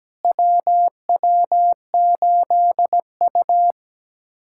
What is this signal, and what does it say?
Morse code, heard as a keyed tone, about 17 wpm, 700 Hz. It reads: WW8U
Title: USA Callsign DE Callsign - 4 Characters - 17wpm